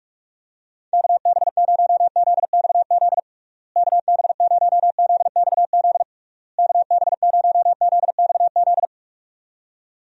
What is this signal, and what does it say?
Morse code, heard as a keyed tone, about 46 wpm, 700 Hz. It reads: KB0ZXZ KB0ZXZ KB0ZXZ
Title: USA Call Signs - 6 Characters - Repeated 3x - 45wpm